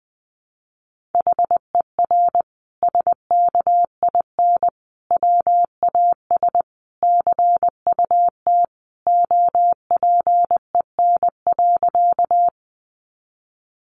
Morse code, heard as a keyed tone, about 20 wpm, 700 Hz. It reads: HER SKIN WAS CUT OPEN.
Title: Sentences from Top 1300 Words 20wpm